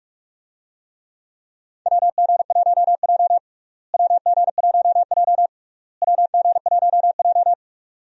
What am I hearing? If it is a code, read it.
Morse code, heard as a keyed tone, about 45 wpm, 700 Hz. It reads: WG1J WG1J WG1J
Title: USA Call Signs - 4 Characters - Repeated 3x - 45wpm